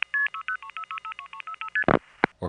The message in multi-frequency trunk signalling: 60363ST